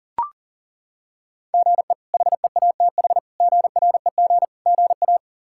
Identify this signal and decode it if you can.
Morse code, heard as a keyed tone, about 40 wpm, 700 Hz. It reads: GE HEATH GREG GA
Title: K1USN SST QSO Practice - 40wpm